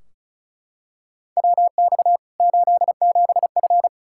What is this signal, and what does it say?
Morse code, heard as a keyed tone, about 36 wpm, 700 Hz. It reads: WX 87F
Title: Practice QSO - Exchange 2 - Info - 35wpm